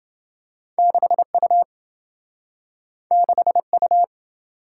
Morse code, heard as a keyed tone, about 28 wpm, 700 Hz. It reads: 6U 6U